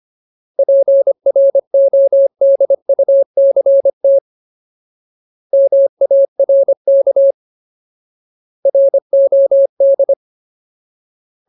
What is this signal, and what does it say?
Morse code, heard as a keyed tone, about 25 wpm, 550 Hz. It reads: PRODUCT MARK ROD
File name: Ultimate Word List / 25WPM Ultimate